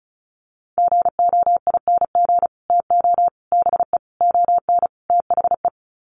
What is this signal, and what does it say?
Morse code, heard as a keyed tone, about 34 wpm, 700 Hz. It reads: GOING TO BE ON THE
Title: Sets of 5 Words Encore2 35wpm